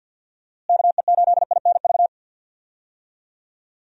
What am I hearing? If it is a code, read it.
Morse code, heard as a keyed tone, about 52 wpm, 700 Hz. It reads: KE8INV
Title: USA Call Signs - 6 Characters - Repeated 3x - 50wpm